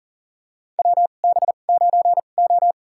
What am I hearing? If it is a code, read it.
Morse code, heard as a keyed tone, about 42 wpm, spaced as for 29 wpm, 700 Hz. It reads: WB9O
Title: USA Call Signs - 4 Characters - 2x Character Spacing - 40wpm